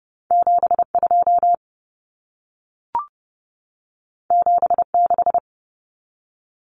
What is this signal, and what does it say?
Morse code, heard as a keyed tone, about 30 wpm, 700 Hz. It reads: 72 76
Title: Common Cut Numbers - 2-Digit 30wpm